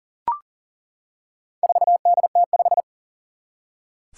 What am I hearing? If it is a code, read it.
Morse code, heard as a keyed tone, about 40 wpm, 700 Hz. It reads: 4DT5
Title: Four Challenging Letter-Number Combinations 40wpm